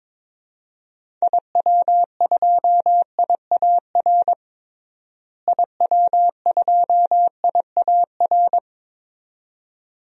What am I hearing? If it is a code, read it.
Morse code, heard as a keyed tone, about 22 wpm, 700 Hz. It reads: IW2IAR IW2IAR
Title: DXCC-100 Call Signs - 6 Characters - Repeated 2x - 22wpm